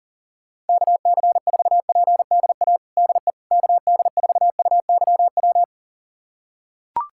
Morse code, heard as a keyed tone, about 40 wpm, 690 Hz. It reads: KC4PDA DE KD4UYW
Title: USA Callsign DE Callsign - 6 Characters - 40wpm